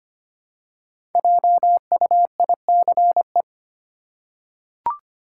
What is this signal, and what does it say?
Morse code, heard as a keyed tone, about 25 wpm, 700 Hz. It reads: JUICE
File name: 5 Letter Words - First 3 Letters Given - 25wpm